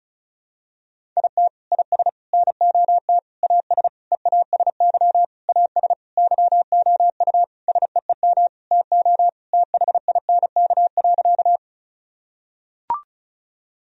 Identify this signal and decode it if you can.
Morse code, heard as a keyed tone, about 36 wpm, 700 Hz. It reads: IT IS NOT AS EASY AS YOU SEEM TO THINK.
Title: Sentences from Top 1000 Words - Review Part 1 - 35wpm